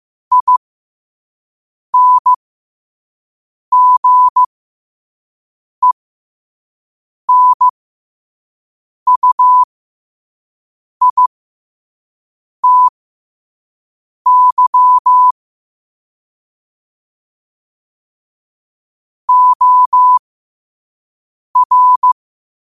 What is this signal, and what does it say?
Morse code, heard as a keyed tone, about 15 wpm, spaced as for 5 wpm, 1000 Hz. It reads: INGENUITY OR
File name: 5wpm